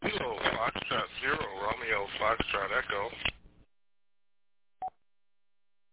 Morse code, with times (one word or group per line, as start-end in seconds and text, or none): none